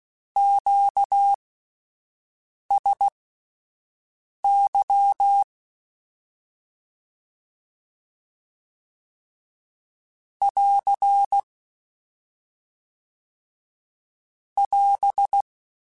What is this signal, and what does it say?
Morse code, heard as a keyed tone, about 16 wpm, spaced as for 5 wpm, 800 Hz. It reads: QSY <AR> <AS>